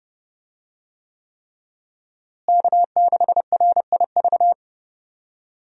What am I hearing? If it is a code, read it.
Morse code, heard as a keyed tone, about 32 wpm, 700 Hz. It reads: K6RIV
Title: USA Call Signs - 5 Characters - Repeated 3x - 30wpm